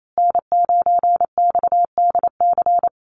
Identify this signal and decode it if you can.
Morse code, heard as a keyed tone, about 28 wpm, 700 Hz. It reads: N9XDC
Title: USA Call Signs Repeated 2x 28wpm